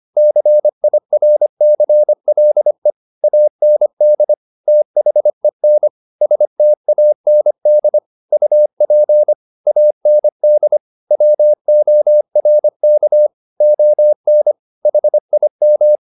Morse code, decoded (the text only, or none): CIRCLE AND THEN STAND UP AND WORK ON HIM